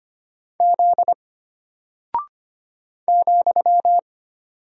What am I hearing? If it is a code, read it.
Morse code, heard as a keyed tone, about 25 wpm, 700 Hz. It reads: Z ,
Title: Single Character (Letter:Number:Punctuation:Prosign) Alt 25wpm